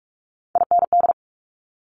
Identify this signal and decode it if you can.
Morse code, heard as a keyed tone, about 46 wpm, 700 Hz. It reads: IND
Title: Multi-letter Phonemes Quick Fire 45wpm